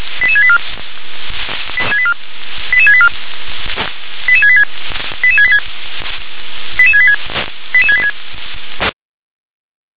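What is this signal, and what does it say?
Five-tone selective call (ZVEI), heard as quick runs of five tones, tones of about 70 ms. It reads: 88684 88684 88686 88686 88686 88686
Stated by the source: ZVEI five-tone call received by SDR.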